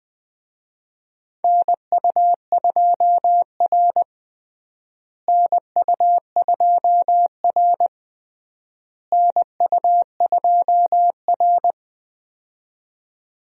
Morse code, heard as a keyed tone, about 20 wpm, 690 Hz. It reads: NU2R NU2R NU2R
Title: USA Call Signs - 4 Characters - Repeated 3x - 20wpm